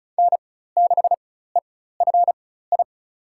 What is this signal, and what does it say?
Morse code, heard as a keyed tone, about 34 wpm, spaced as for 16 wpm, 700 Hz. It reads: N6EFI
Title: USA Call Signs - 5 Characters - 4x Character Spacing - 35wpm